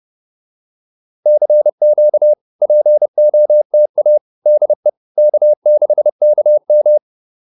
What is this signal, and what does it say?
Morse code, heard as a keyed tone, about 30 wpm, 600 Hz. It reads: CQ POTA DE K6KM